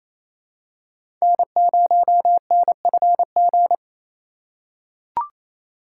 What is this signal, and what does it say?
Morse code, heard as a keyed tone, about 27 wpm, 700 Hz. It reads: N0NFG
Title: USA Call Signs - 5 Characters - 6x Character Spacing - 28wpm